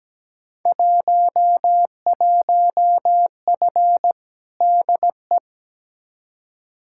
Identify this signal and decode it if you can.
Morse code, heard as a keyed tone, about 17 wpm, 700 Hz. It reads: 11F DE
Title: Field Day Exchanges Encore - 17wpm